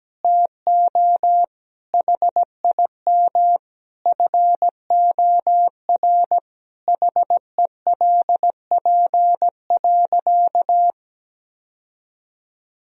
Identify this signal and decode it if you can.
Morse code, heard as a keyed tone, about 17 wpm, 700 Hz. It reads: TO HIM FOR HELP.
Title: Sentences from Top 1000 Words - Review Part 2 - 17wpm